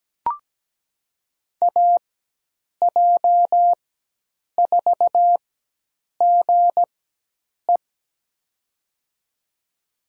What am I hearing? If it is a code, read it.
Morse code, heard as a keyed tone, about 17 wpm, spaced as for 8 wpm, 700 Hz. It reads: AJ4GE
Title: USA Call Signs - 4x Character Spacing - 17wpm